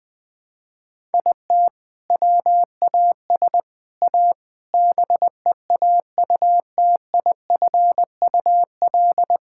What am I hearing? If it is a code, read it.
Morse code, heard as a keyed tone, about 20 wpm, 700 Hz. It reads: IT WAS A BEAUTIFUL
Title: Sentences from Top 1000 Words - Review Part 5 - 20wpm